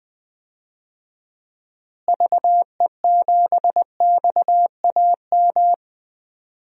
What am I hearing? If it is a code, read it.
Morse code, heard as a keyed tone, about 20 wpm, 700 Hz. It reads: VE7XAM
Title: DXCC-100 Call Signs - 6 Characters - Repeated 3x - 20wpm